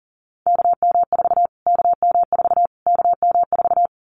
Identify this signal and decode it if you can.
Morse code, heard as a keyed tone, about 40 wpm, 700 Hz. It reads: KM4 KM4 KM4
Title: Callsign Prefixes for Top 25 DXCC - Repeated 3x - 40wpm